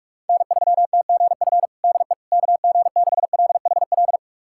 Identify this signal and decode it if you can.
Morse code, heard as a keyed tone, about 42 wpm, 700 Hz. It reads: N3TGF DE KG6LHL
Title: USA Callsign DE Callsign 45wpm